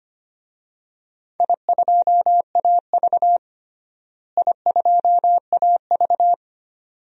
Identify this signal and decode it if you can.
Morse code, heard as a keyed tone, about 25 wpm, 700 Hz. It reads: I2AV I2AV
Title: DXCC-100 Call Signs - 4 Characters - Repeated 2x - 25wpm